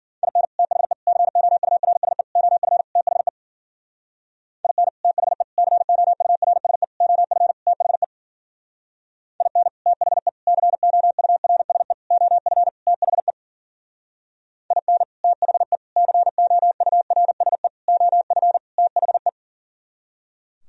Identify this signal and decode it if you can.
Morse code, heard as a keyed tone, about 51 wpm, 700 Hz. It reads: IN THE COURSE OF THE IN THE COURSE OF THE IN THE COURSE OF THE IN THE COURSE OF THE